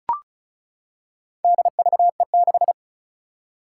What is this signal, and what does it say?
Morse code, heard as a keyed tone, about 36 wpm, 700 Hz. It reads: DVE6
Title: Four Challenging Letter-Number Combinations 35wpm